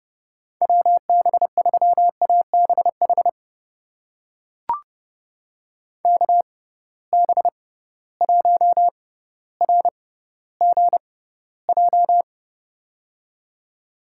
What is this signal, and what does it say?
Morse code, heard as a keyed tone, about 30 wpm, spaced as for 15 wpm, 700 Hz. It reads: WB3ABH KB1RGJ
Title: USA Call Signs - 6 Characters - 6x Character Spacing - 30wpm